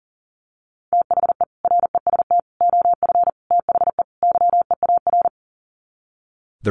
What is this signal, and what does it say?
Morse code, heard as a keyed tone, about 40 wpm, 700 Hz. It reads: THE REST OF THE YEAR
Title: Sets of 5 Words Encore 40wpm